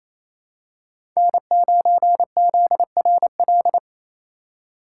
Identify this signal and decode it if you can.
Morse code, heard as a keyed tone, about 27 wpm, 700 Hz. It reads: N9ZRL
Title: USA Call Signs - 3x Character Spacing - 28wpm